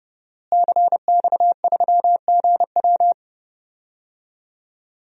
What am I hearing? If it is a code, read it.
Morse code, heard as a keyed tone, about 30 wpm, 700 Hz. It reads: CX3GW
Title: DXCC-100 Call Signs - 5 Characters - Repeated 2x - 30wpm